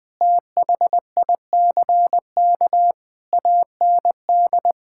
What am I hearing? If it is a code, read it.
Morse code, heard as a keyed tone, about 20 wpm, 700 Hz. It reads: THICK AND
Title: Sentences from Top 1600 Words 20wpm